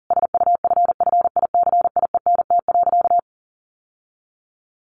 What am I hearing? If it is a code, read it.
Morse code, heard as a keyed tone, about 40 wpm, 700 Hz. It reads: SUFFICIENT.
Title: Sentences from Top 1400 Words 40wpm